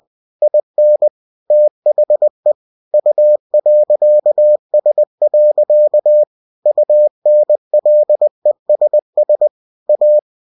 Morse code, heard as a keyed tone, about 20 wpm, 600 Hz. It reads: IN THE U.S. UNLESS A
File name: cw-books / winnie-the-pooh / winnie-20wpm-page-0122